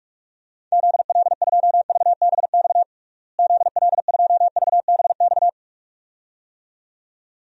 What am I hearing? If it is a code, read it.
Morse code, heard as a keyed tone, about 41 wpm, 700 Hz. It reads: ZL2VBX ZL2VBX
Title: DXCC-100 Call Signs - 6 Characters - Repeated 2x - 45wpm